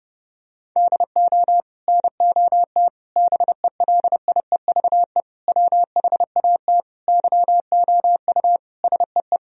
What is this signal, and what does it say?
Morse code, heard as a keyed tone, about 32 wpm, 700 Hz. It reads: DO NOT BELIEVE WHAT YOU SEE
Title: Sentences from Top 1000 Words - Review Part 4 - 30wpm